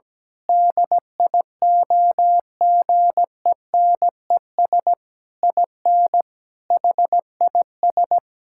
Morse code, heard as a keyed tone, about 17 wpm, 700 Hz. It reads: DIOGENES IN HIS
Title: Pangram Challenge v2 - 17wpm